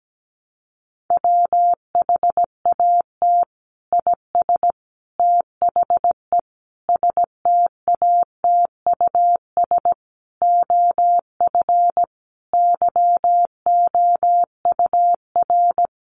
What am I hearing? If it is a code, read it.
Morse code, heard as a keyed tone, about 17 wpm, 700 Hz. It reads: WHAT IS THE STATUS OF YOUR
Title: Sentences from Top 1400 Words 17wpm